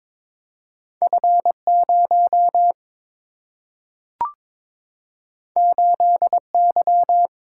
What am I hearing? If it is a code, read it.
Morse code, heard as a keyed tone, about 22 wpm, 700 Hz. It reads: F 0 8 Y
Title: Single-Letter Single-Number 22wpm